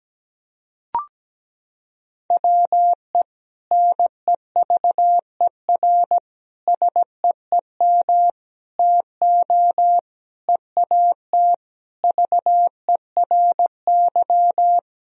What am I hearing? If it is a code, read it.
Morse code, heard as a keyed tone, about 17 wpm, 700 Hz. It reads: WE NEVER SEEM TO EAT VERY
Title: Sentences from Top 1000 Words - Review Part 1 - 17wpm